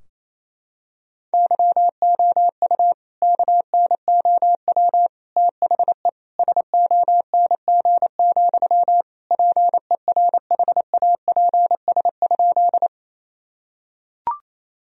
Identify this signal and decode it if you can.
Morse code, heard as a keyed tone, about 28 wpm, 700 Hz. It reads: YOU KNOW THE SONG, PERHAPS?
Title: Sentences from Top 1000 Words - Review Part 2 - 28wpm